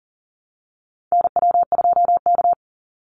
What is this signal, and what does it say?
Morse code, heard as a keyed tone, about 40 wpm, 700 Hz. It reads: NW2K
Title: Call Signs Repeated 2x - 40wpm